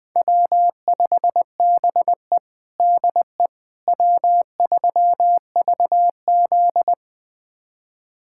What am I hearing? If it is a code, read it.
Morse code, heard as a keyed tone, about 21 wpm, 700 Hz. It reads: W5BE DE W3VZ
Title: USA Callsign DE Callsign - 4 Characters - 20wpm